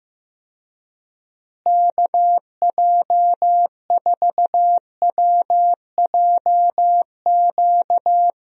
Morse code, heard as a keyed tone, about 15 wpm, 700 Hz. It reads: KJ4WJQ
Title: USA Call Signs - 6 Characters - Repeated 2x - 15wpm